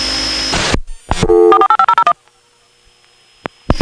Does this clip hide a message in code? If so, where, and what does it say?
Yes: DTMF from 1.5 to 2.1 s, heard as telephone touch tones, tones of about 80 ms. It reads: 48501